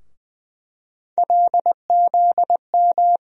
Morse code, heard as a keyed tone, about 20 wpm, 700 Hz. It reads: LZM
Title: USA Call Sign Postfixes - 3 Letters 20wpm